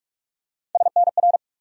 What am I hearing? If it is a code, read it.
Morse code, heard as a keyed tone, about 44 wpm, 700 Hz. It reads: SNR
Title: USA Call Sign Postfixes - 3 Letters 45wpm